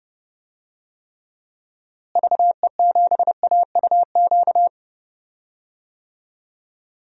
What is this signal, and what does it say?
Morse code, heard as a keyed tone, about 30 wpm, 700 Hz. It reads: VE7AUQ
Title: DXCC-100 Call Signs - 6 Characters - Repeated 3x - 30wpm